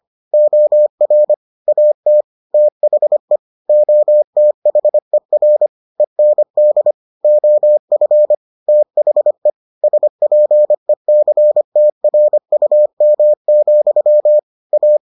Morse code, heard as a keyed tone, about 25 wpm, 600 Hz. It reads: OR AT THE OTHER END OF THE SPECTRUM, A